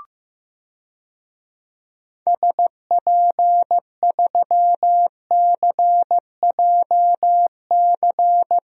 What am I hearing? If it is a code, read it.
Morse code, heard as a keyed tone, about 15 wpm, 700 Hz. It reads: SP3CJC
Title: DXCC-100 Call Signs - 6 Characters - Repeated 3x - 15wpm